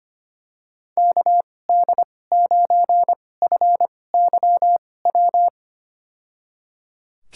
Morse code, heard as a keyed tone, about 25 wpm, spaced as for 18 wpm, 700 Hz. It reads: KD9FYW
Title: USA Call Signs - 6 Characters - 2x Character Spacing - 25wpm